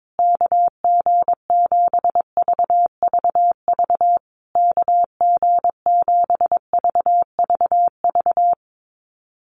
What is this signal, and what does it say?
Morse code, heard as a keyed tone, about 22 wpm, 690 Hz. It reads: KG7VVV KG7VVV
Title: USA Call Signs Repeated 3x 22wpm